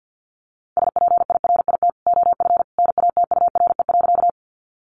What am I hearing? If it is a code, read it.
Morse code, heard as a keyed tone, about 50 wpm, 700 Hz. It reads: SPIRIT OF NATURE.
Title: Sentences from Top 1400 Words 50wpm